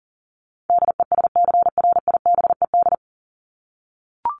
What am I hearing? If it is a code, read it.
Morse code, heard as a keyed tone, about 40 wpm, 700 Hz. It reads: DESCRIBED
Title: ED Suffix - Top 100 Words with Base Words 40wpm